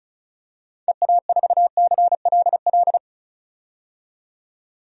Morse code, heard as a keyed tone, about 36 wpm, 700 Hz. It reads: EA4CLL
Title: DXCC-100 Call Signs - 6 Characters - 35wpm